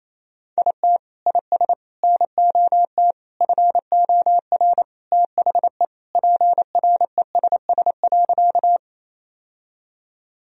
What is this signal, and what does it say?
Morse code, heard as a keyed tone, about 28 wpm, 700 Hz. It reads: IT IS NOT FOR THE PRESS.
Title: Sentences from Top 1000 Words - Review Part 1 - 28wpm